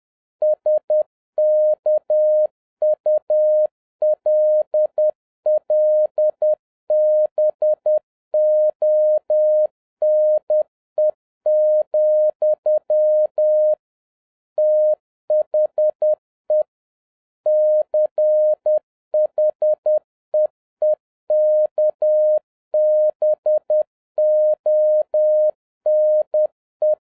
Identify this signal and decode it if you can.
Morse code, heard as a keyed tone, about 10 wpm, 610 Hz. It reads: SKULLBONE, THE CHEEKBONE
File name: 2011-07-mp3 / n56 BACK FROM A BIG ACCIDENT 10wpm